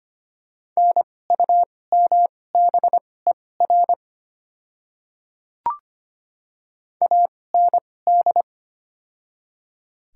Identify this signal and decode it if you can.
Morse code, heard as a keyed tone, about 25 wpm, spaced as for 18 wpm, 690 Hz. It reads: NUMBER AND
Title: 2x Character Spacing - Top 100 Words - 25wpm